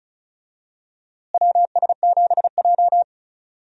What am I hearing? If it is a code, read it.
Morse code, heard as a keyed tone, about 36 wpm, 700 Hz. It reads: WS7J
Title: USA Call Signs - 4 Characters - Repeated 3x - 35wpm